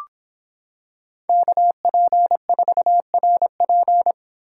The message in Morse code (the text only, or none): KP4RP